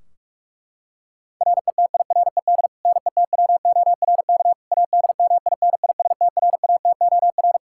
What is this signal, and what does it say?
Morse code, heard as a keyed tone, about 46 wpm, 700 Hz. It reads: RETIRED NETWORK ADMINISTRATOR
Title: K1USN SST QSO Practice 45wpm